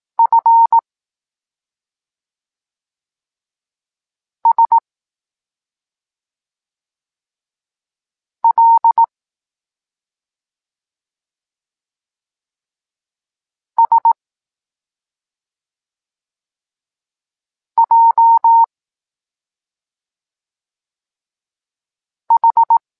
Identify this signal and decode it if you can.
Morse code, heard as a keyed tone, about 18 wpm, 920 Hz. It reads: F S L S J H
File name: Tekst-26 1